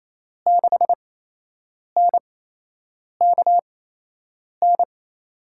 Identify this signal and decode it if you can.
Morse code, heard as a keyed tone, about 28 wpm, spaced as for 8 wpm, 700 Hz. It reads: 6NKN
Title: USA Call Signs - 6 Characters - 8x Character Spacing - 28wpm